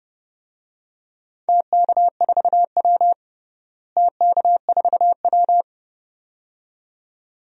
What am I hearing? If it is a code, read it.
Morse code, heard as a keyed tone, about 30 wpm, 700 Hz. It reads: TK4W TK4W
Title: DXCC-100 Call Signs - 4 Characters - Repeated 2x - 30wpm